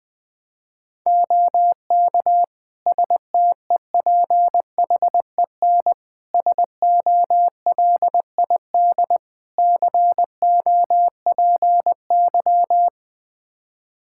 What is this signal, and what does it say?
Morse code, heard as a keyed tone, about 20 wpm, 700 Hz. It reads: OK STEPHEN SOLID COPY
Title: K1USN SST QSO Practice 20wpm